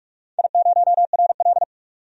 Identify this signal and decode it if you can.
Morse code, heard as a keyed tone, about 41 wpm, 700 Hz. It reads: I0RL
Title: DXCC-100 Call Signs - 4 Characters - 45wpm